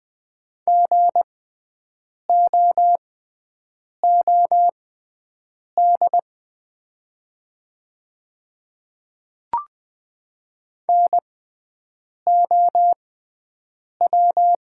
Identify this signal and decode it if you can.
Morse code, heard as a keyed tone, about 20 wpm, spaced as for 7 wpm, 700 Hz. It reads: GOOD NOW
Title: 6x Character Spacing - Top 500 Words - 20wpm